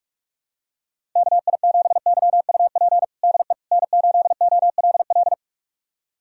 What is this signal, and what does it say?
Morse code, heard as a keyed tone, about 44 wpm, 690 Hz. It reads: KI7YUP DE N8OLL
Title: USA Callsign DE Callsign 45wpm